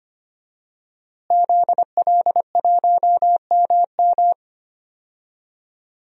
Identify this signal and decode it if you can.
Morse code, heard as a keyed tone, about 25 wpm, 700 Hz. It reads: ZL1MM
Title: DXCC-100 Call Signs - 5 Characters - Repeated 3x - 25wpm